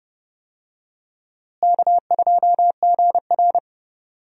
Morse code, heard as a keyed tone, about 32 wpm, 700 Hz. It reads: K2GR
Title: USA Call Signs - 4 Characters - Repeated 2x - 30wpm